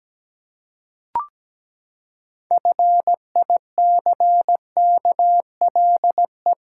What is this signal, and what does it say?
Morse code, heard as a keyed tone, about 17 wpm, 700 Hz. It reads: FICKLE
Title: Pangram Challenge v2 - 17wpm